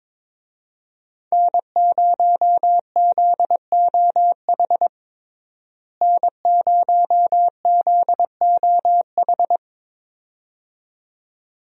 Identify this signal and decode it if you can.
Morse code, heard as a keyed tone, about 22 wpm, 700 Hz. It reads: N0ZOH N0ZOH
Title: USA Call Signs - 5 Characters - Repeated 2x - 22wpm